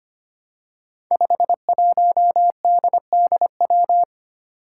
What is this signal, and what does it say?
Morse code, heard as a keyed tone, about 25 wpm, 700 Hz. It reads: 51DDW
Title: Code Groups - 5 Letter:Numbers - 25wpm